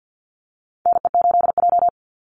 Morse code, heard as a keyed tone, about 50 wpm, 700 Hz. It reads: NE8J
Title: Call Signs Repeated 3x - 50wpm